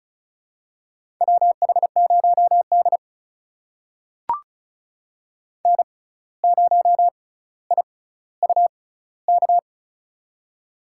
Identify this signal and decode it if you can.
Morse code, heard as a keyed tone, about 36 wpm, spaced as for 17 wpm, 700 Hz. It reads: WH0D N0IUK
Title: USA Call Signs - 6x Character Spacing - 35wpm